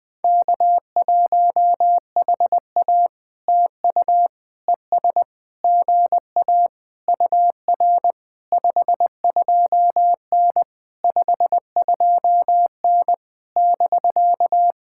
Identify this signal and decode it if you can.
Morse code, heard as a keyed tone, about 21 wpm, 700 Hz. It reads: K1HA TU ES GA UR 52N 52N <BK>
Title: POTA QSO Practice - 20wpm